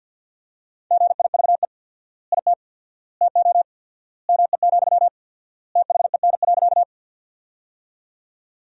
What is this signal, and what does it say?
Morse code, heard as a keyed tone, about 50 wpm, 700 Hz. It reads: GIVE IT TO ME, THEN.
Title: 4x Word-Spacing - Sentences from Top 200 Words - 50wpm